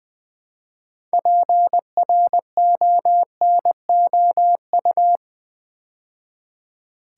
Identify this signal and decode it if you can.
Morse code, heard as a keyed tone, about 20 wpm, 700 Hz. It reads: PRONOU
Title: Build A Word - 3 Letters At A Time - 9-Letter Words - 20wpm